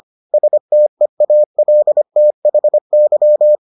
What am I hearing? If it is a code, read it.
Morse code, heard as a keyed tone, about 25 wpm, 600 Hz. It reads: STEALTHY